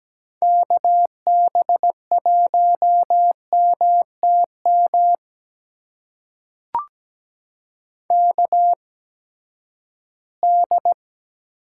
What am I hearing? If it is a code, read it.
Morse code, heard as a keyed tone, about 17 wpm, spaced as for 12 wpm, 700 Hz. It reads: KB1MTM KD
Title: USA Call Signs - 6 Characters - 8x Character Spacing - 17wpm